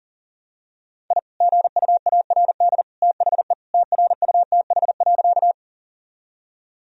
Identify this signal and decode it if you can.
Morse code, heard as a keyed tone, about 40 wpm, 700 Hz. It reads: I GUARD THE TRUTH.
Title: Sentences from Top 1000 Words - Review Part 4 - 40wpm